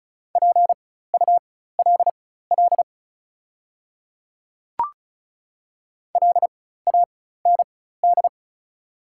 Morse code, heard as a keyed tone, about 34 wpm, spaced as for 16 wpm, 690 Hz. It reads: PULL LAND
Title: 4x Character Spacing - Top 500 Words - 35wpm